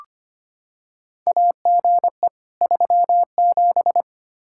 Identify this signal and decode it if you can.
Morse code, heard as a keyed tone, about 25 wpm, 700 Hz. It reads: AGE 37
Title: Practice QSO - Exchange 3 - More Info - 25wpm